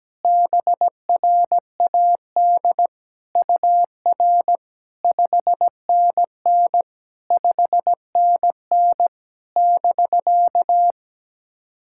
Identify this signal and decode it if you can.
Morse code, heard as a keyed tone, about 17 wpm, 700 Hz. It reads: BRAD UR 5NN 5NN <BK>
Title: SOTA QSO Practice 17wpm